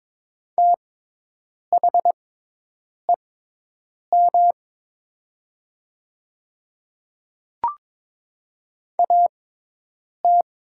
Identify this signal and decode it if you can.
Morse code, heard as a keyed tone, about 22 wpm, spaced as for 8 wpm, 700 Hz. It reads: THEM AT
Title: 6x Character Spacing - Top 300 Words - 22wpm